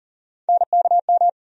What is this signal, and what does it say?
Morse code, heard as a keyed tone, about 40 wpm, 700 Hz. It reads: NKM
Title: USA Call Sign Postfixes - 3 Letters 40wpm